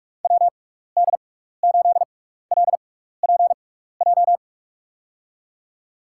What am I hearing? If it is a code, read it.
Morse code, heard as a keyed tone, about 46 wpm, spaced as for 16 wpm, 700 Hz. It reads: WD8LPJ
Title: USA Call Signs - 6 Characters - 6x Character Spacing - 45wpm